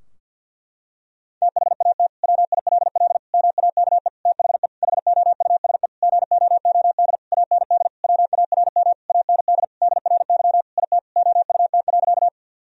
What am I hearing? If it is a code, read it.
Morse code, heard as a keyed tone, about 50 wpm, 700 Hz. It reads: THAT WILL MAKE THE HOUSE GOOD AND WARM AND DRY IT OUT.
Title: Sentences from Top 1000 Words - Review Part 3 - 50wpm